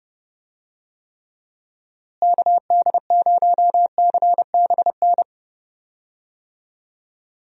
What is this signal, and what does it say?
Morse code, heard as a keyed tone, about 30 wpm, 700 Hz. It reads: KD0CBN
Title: USA Call Signs - 6 Characters - Repeated 3x - 30wpm